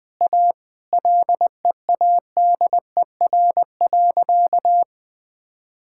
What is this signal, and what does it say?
Morse code, heard as a keyed tone, about 21 wpm, 700 Hz. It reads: A LEADER.
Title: Sentences with 2-Word Phrases - 20wpm